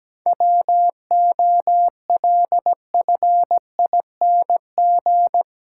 Morse code, heard as a keyed tone, about 17 wpm, 700 Hz. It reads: WOLFING
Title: Pangram Challenge v2 - 17wpm